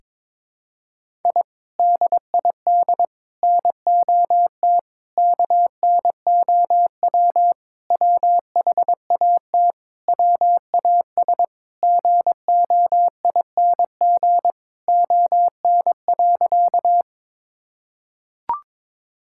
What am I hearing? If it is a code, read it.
Morse code, heard as a keyed tone, about 22 wpm, 700 Hz. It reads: I DID NOT KNOW WHAT WAS GOING ON.
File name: practice / 22wpm / Sentences with 4-Word Phrases - 22wpm